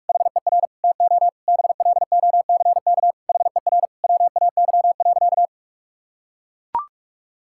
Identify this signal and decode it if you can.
Morse code, heard as a keyed tone, about 42 wpm, 700 Hz. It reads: HER TO BLOCK HER WAY.
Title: Sentences from Top 1600 Words 45wpm